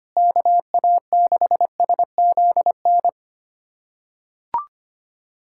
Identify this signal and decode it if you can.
Morse code, heard as a keyed tone, about 25 wpm, 700 Hz. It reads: KA6SZN
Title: USA Callsign DE Callsign - 6 Characters - 25wpm